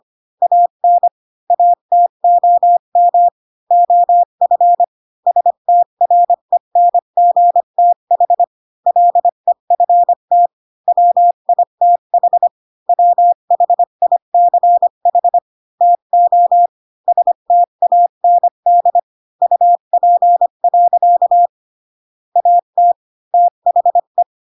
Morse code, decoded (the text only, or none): AN ATOM OF STRENGTH LEFT WITH WHICH TO STAND UP. AT THE